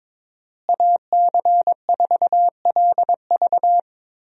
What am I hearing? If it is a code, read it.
Morse code, heard as a keyed tone, about 22 wpm, 700 Hz. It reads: AC4LV
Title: USA Call Signs - 5 Characters - Repeated 2x - 22wpm